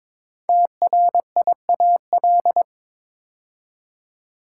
Morse code, heard as a keyed tone, about 22 wpm, 700 Hz. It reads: TRIAL
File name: 5 Letter Words - First 3 Letters Given - 22wpm